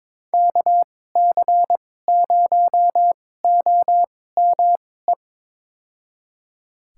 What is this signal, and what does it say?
Morse code, heard as a keyed tone, about 21 wpm, spaced as for 16 wpm, 700 Hz. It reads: KC0OME
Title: USA Call Signs - 6 Characters - 2x Character Spacing - 22wpm